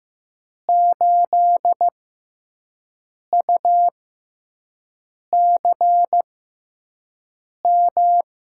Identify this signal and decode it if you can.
Morse code, heard as a keyed tone, about 15 wpm, spaced as for 5 wpm, 700 Hz. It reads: 8UCM
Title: USA Call Signs - 6x Character Spacing - 15wpm